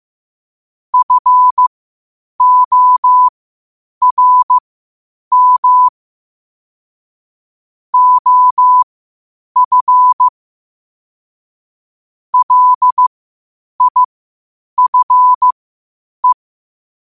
Morse code, heard as a keyed tone, about 15 wpm, spaced as for 8 wpm, 1000 Hz. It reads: FORM OF LIFE
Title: 8 WPM morse code quotes for Fri, 01 Mar 2024